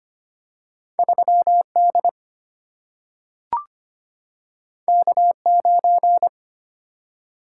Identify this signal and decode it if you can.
Morse code, heard as a keyed tone, about 25 wpm, 700 Hz. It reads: 3D K9
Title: Single-Letter Single-Number 25wpm